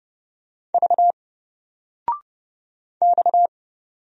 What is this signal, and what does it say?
Morse code, heard as a keyed tone, about 30 wpm, 700 Hz. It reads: V X
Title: Single Letter-Number 30wpm